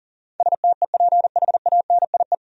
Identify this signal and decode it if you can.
Morse code, heard as a keyed tone, about 40 wpm, 700 Hz. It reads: STEPHANIE
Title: USA Common Names - 40wpm